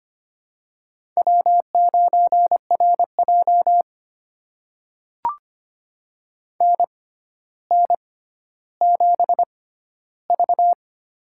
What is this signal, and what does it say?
Morse code, heard as a keyed tone, about 25 wpm, spaced as for 13 wpm, 700 Hz. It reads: W9RJ NN7V
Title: USA Call Signs - 4 Characters - 6x Character Spacing - 25wpm